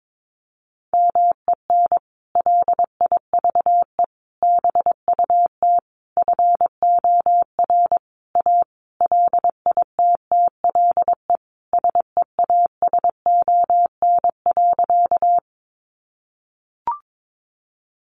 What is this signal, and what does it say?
Morse code, heard as a keyed tone, about 22 wpm, 700 Hz. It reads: MEN LIVE BUT FOR A LITTLE SEASON.
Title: Sentences from Top 1400 Words 22wpm